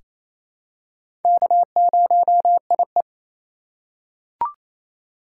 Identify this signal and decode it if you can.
Morse code, heard as a keyed tone, about 28 wpm, 690 Hz. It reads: K0IE
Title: USA Call Signs - 2x Character Spacing - 28wpm